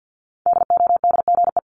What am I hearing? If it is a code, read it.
Morse code, heard as a keyed tone, about 50 wpm, 690 Hz. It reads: DODGE
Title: Multi-letter Phonemes 50wpm